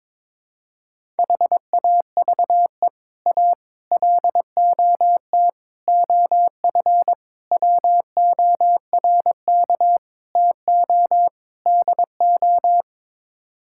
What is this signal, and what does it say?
Morse code, heard as a keyed tone, about 22 wpm, 700 Hz. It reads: HAVE A LOT OF WORK TO DO
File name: practice / 22wpm / Sets of 7 Words - 22wpm